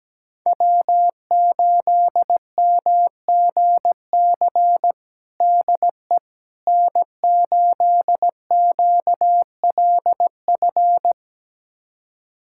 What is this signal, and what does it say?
Morse code, heard as a keyed tone, about 17 wpm, 700 Hz. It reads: W8MGC DE N8QLF
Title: USA Callsign DE Callsign - 5 Characters - 17wpm